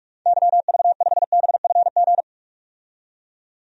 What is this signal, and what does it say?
Morse code, heard as a keyed tone, about 42 wpm, 700 Hz. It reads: YV5BFG
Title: DXCC-100 Call Signs - 6 Characters - 45wpm